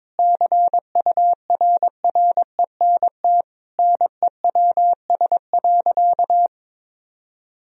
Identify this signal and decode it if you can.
Morse code, heard as a keyed tone, about 22 wpm, 700 Hz. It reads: CURRENT NEWS.
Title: Sentences from Top 1000 Words - Review Part 4 - 22wpm